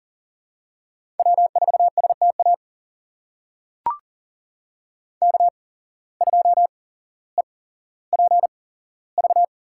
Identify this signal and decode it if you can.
Morse code, heard as a keyed tone, about 42 wpm, spaced as for 17 wpm, 700 Hz. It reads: W4STA K2EPV
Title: USA Call Signs - 5 Characters - 8x Character Spacing - 40wpm